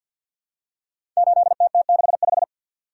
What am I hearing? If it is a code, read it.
Morse code, heard as a keyed tone, about 46 wpm, 700 Hz. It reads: 8TT65
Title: Code Groups - 5 Letter:Numbers - 50wpm